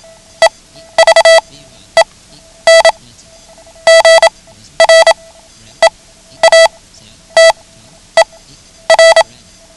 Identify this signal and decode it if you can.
Morse code, heard as a keyed tone, about 27 wpm, spaced as for 11 wpm, 660 Hz. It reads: EVEN GREATER
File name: ham radio / CodeQuick / Code Quick 09